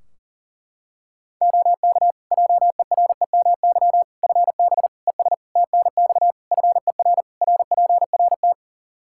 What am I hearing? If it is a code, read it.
Morse code, heard as a keyed tone, about 37 wpm, 700 Hz. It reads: OK JEREMY FB ES TNX FER RPRT
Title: Practice QSO - Exchange 2 - Info - 40wpm